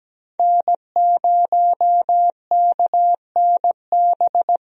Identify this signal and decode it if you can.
Morse code, heard as a keyed tone, about 17 wpm, 700 Hz. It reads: N0KNB
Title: USA Call Signs - 5 Characters - Repeated 2x - 17wpm